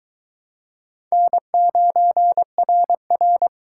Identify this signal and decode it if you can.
Morse code, heard as a keyed tone, about 22 wpm, 700 Hz. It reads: N9RR